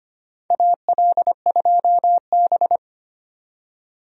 Morse code, heard as a keyed tone, about 25 wpm, 700 Hz. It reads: AL2B
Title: USA Call Signs - 4 Characters - 25wpm